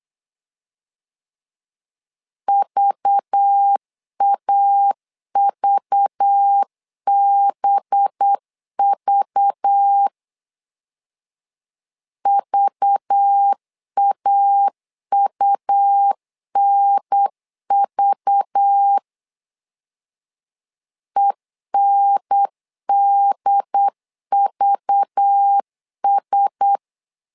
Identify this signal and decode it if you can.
Morse code, heard as a keyed tone, about 8 wpm, 800 Hz. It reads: VAVBV VAUNV ENDVS
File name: LEZIONI / 5 LBVC / 5 LBV 8wpm 2837